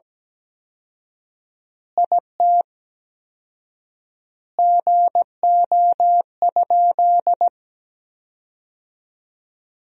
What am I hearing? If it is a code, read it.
Morse code, heard as a keyed tone, about 17 wpm, 700 Hz. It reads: IT GO?
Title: 4x Word-Spacing - Sentences from Top 200 Words - 17wpm